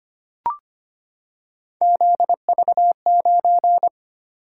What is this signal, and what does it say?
Morse code, heard as a keyed tone, about 25 wpm, 700 Hz. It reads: ZV9
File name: Three Letter-Number Combinations - 25wpm